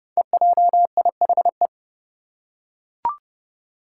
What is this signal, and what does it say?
Morse code, heard as a keyed tone, about 30 wpm, 700 Hz. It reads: EJIHE
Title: Five Challenging Letter-Number Combinations 30wpm